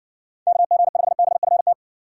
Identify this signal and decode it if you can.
Morse code, heard as a keyed tone, about 55 wpm, 700 Hz. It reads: KG5BFT